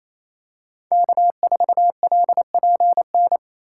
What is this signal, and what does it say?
Morse code, heard as a keyed tone, about 28 wpm, 700 Hz. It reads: K4LPN
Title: USA Call Signs - 5 Characters - 4x Character Spacing - 28wpm